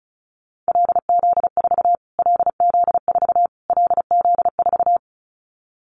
Morse code, heard as a keyed tone, about 34 wpm, 700 Hz. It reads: LZ4 LZ4 LZ4
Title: Callsign Prefixes for Top 25 DXCC - Repeated 3x - 35wpm